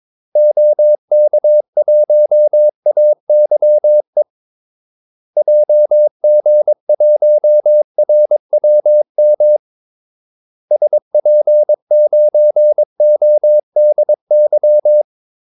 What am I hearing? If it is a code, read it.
Morse code, heard as a keyed tone, about 22 wpm, 600 Hz. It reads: OK1AYE JG1RWM SP9ODY
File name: Callsigns-22wpm